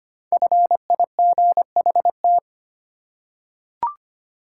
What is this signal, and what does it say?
Morse code, heard as a keyed tone, about 25 wpm, 700 Hz. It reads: FIGHT
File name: 5 Letter Words - First 3 Letters Given - 25wpm